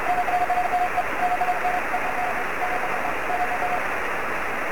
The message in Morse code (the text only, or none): none